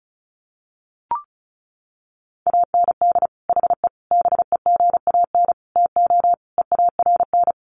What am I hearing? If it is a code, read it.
Morse code, heard as a keyed tone, about 34 wpm, 700 Hz. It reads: AND HE BEGAN TO EARN
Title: Sentences from Top 1500 Words 35wpm